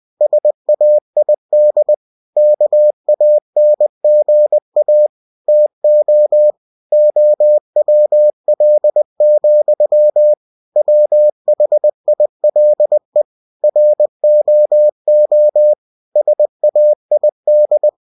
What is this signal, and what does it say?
Morse code, heard as a keyed tone, about 20 wpm, 600 Hz. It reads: SAID KANGA TO OWL, WHILE ROO SAID
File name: cw-books / winnie-the-pooh / winnie-20wpm-page-0081